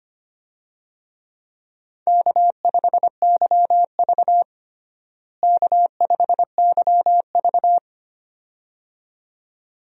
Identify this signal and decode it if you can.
Morse code, heard as a keyed tone, about 25 wpm, 700 Hz. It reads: K5YV K5YV
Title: USA Call Signs - 4 Characters - Repeated 2x - 25wpm